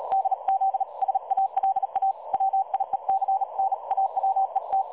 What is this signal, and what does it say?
Morse code, heard as a keyed tone, about 36 wpm, 770 Hz. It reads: DZ SALA WIEC NAWET
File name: dlazaawansowanych / nagraniazpasma / websdr recording 3565.0kHz